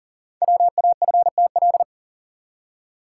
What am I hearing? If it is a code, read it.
Morse code, heard as a keyed tone, about 39 wpm, 700 Hz. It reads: WAFTL
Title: Code Groups - 5 Letters - 40wpm